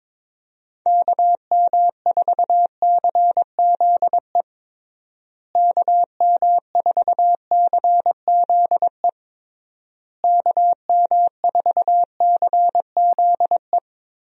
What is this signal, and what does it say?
Morse code, heard as a keyed tone, about 21 wpm, 700 Hz. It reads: KM4CZE KM4CZE KM4CZE
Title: USA Call Signs - 6 Characters - Repeated 3x - 22wpm